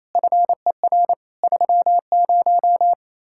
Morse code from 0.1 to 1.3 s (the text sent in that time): FER